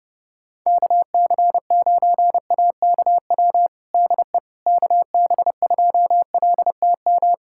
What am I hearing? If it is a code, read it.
Morse code, heard as a keyed tone, about 30 wpm, 700 Hz. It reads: KC9AKW DE KB2LTM
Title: USA Callsign DE Callsign 30wpm